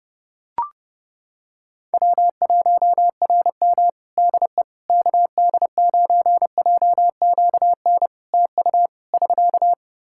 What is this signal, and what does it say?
Morse code, heard as a keyed tone, about 32 wpm, 700 Hz. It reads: W1RM DE KD9JQN TU <SK>
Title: K1USN SST QSO Practice 30wpm